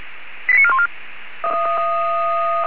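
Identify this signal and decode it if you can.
Five-tone selective call (ZVEI), heard as quick runs of five tones, tones of about 70 ms. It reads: 89414
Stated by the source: ZVEI five-tone selective call received via SDR.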